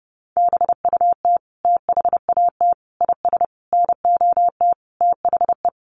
Morse code, heard as a keyed tone, about 30 wpm, 700 Hz. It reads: BUT THAT IS NOT THE
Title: Sentences from Top 600 Words 30wpm